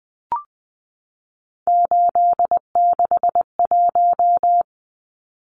Morse code, heard as a keyed tone, about 20 wpm, 700 Hz. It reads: 861
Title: Numbers - 3-Digit 20wpm